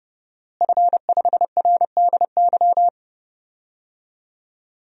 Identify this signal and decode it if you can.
Morse code, heard as a keyed tone, about 30 wpm, 700 Hz. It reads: F5RDY
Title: DXCC-100 Call Signs - 5 Characters - Repeated 3x - 30wpm